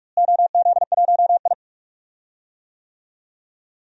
Morse code, heard as a keyed tone, about 46 wpm, 700 Hz. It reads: OZ1I
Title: DXCC-100 Call Signs - 4 Characters - Repeated 2x - 45wpm